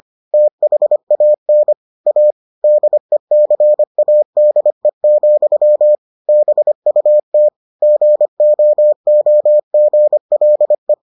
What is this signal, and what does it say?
Morse code, heard as a keyed tone, about 25 wpm, 600 Hz. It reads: THAN A DECADE, BUT GOOGLE